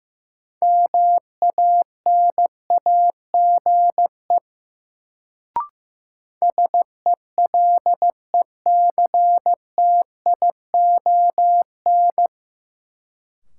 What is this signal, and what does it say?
Morse code, heard as a keyed tone, about 15 wpm, 700 Hz. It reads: MANAGE SELECTION
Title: Sentences from Top 1400 Words - New Word Drill Alt - 15wpm